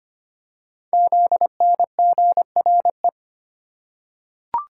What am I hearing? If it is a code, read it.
Morse code, heard as a keyed tone, about 25 wpm, 700 Hz. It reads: ZNGRE
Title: Code Groups - 5 Letters - 25wpm